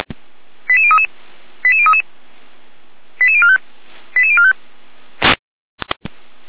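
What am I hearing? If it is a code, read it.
Five-tone selective call (ZVEI), heard as quick runs of five tones, tones of about 70 ms. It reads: 88033 88033 88045 88045
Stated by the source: ZVEI five-tone selective call received by SDR, from audio.f85360.feuerwehr-orlamuende.org fire brigade server.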